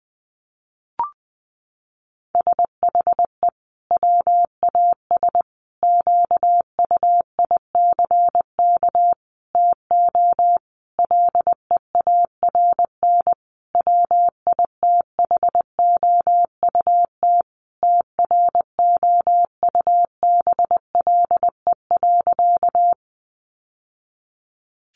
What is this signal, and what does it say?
Morse code, heard as a keyed tone, about 20 wpm, 700 Hz. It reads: SHE WAS QUICK TO LEARN WITHOUT TROUBLE.
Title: Sentences from Top 1200 Words 20wpm